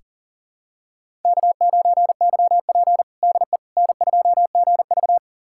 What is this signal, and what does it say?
Morse code, heard as a keyed tone, about 40 wpm, 700 Hz. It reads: K9YP DE N2GV
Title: USA Callsign DE Callsign - 4 Characters - 40wpm